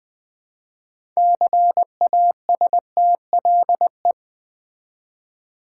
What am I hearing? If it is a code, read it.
Morse code, heard as a keyed tone, about 20 wpm, 700 Hz. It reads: CASTLE
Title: 6 Letter Words - First 3 Letters Given - 20wpm